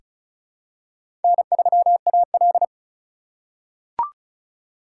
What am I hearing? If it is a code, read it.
Morse code, heard as a keyed tone, about 34 wpm, 700 Hz. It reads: N3AL
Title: USA Call Signs - 4 Characters - 2x Character Spacing - 35wpm